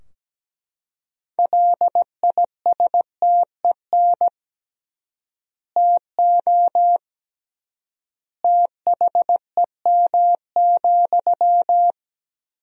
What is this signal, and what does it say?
Morse code, heard as a keyed tone, about 17 wpm, 700 Hz. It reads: LISTEN TO THEM,
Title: 3x Word-Spacing - Sentences from Top 400 Words - 17wpm